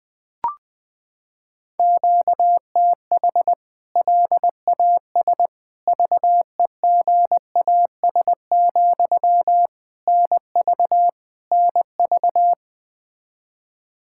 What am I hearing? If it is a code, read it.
Morse code, heard as a keyed tone, about 20 wpm, 700 Hz. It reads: QTH LAS VEGAS, NV NV
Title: K1USN SST QSO Practice 20wpm